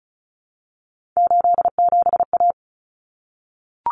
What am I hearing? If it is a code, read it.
Morse code, heard as a keyed tone, about 34 wpm, 700 Hz. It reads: 87A
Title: Common Cut Numbers - 3-Digit 35wpm